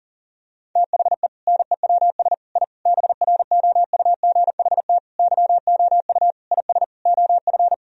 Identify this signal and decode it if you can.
Morse code, heard as a keyed tone, about 40 wpm, 700 Hz. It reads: THE NEWS I BROUGHT YOU IS OF
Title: Sentences from Top 1000 Words - Review Part 3 - 40wpm